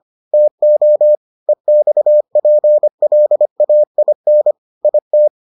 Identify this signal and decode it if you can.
Morse code, heard as a keyed tone, about 25 wpm, 600 Hz. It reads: TO EXPLAIN IT